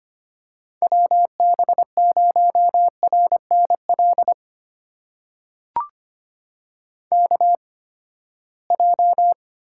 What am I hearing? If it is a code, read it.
Morse code, heard as a keyed tone, about 25 wpm, spaced as for 17 wpm, 700 Hz. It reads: WB0RNL KJ
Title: USA Call Signs - 6 Characters - 8x Character Spacing - 25wpm